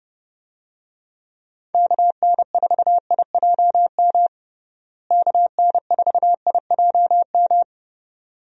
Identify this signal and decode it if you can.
Morse code, heard as a keyed tone, about 31 wpm, 700 Hz. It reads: KN4IJM KN4IJM
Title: USA Call Signs - 6 Characters - Repeated 2x - 30wpm